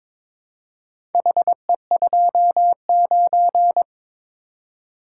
Morse code, heard as a keyed tone, about 22 wpm, 700 Hz. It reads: HE29
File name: practice / 22wpm / Four Letter-Number Combinations - 22wpm